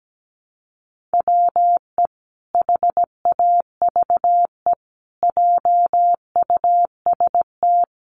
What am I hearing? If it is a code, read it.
Morse code, heard as a keyed tone, about 17 wpm, 700 Hz. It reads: WE HAVE JUST
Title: Sentences from Top 900 Words 17wpm